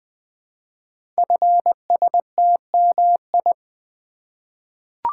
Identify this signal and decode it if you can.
Morse code, heard as a keyed tone, about 21 wpm, 700 Hz. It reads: FSTMI
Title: Code Groups - 5 Letters - 20wpm